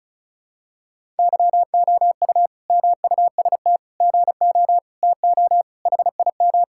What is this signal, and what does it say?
Morse code, heard as a keyed tone, about 34 wpm, 700 Hz. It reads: YOU MUST GO TO HIM
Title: Sentences from Top 1000 Words - Review Part 5 - 35wpm